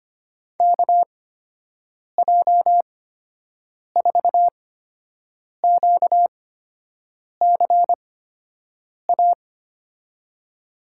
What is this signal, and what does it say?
Morse code, heard as a keyed tone, about 25 wpm, spaced as for 7 wpm, 700 Hz. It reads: KJ4QCA
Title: USA Call Signs - 6 Characters - 8x Character Spacing - 25wpm